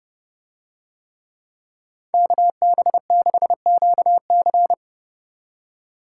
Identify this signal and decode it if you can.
Morse code, heard as a keyed tone, about 27 wpm, 700 Hz. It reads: KB6QC